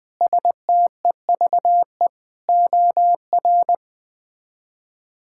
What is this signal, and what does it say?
Morse code, heard as a keyed tone, about 21 wpm, 700 Hz. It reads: STEVE OR
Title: K1USN SST QSO Practice - 20wpm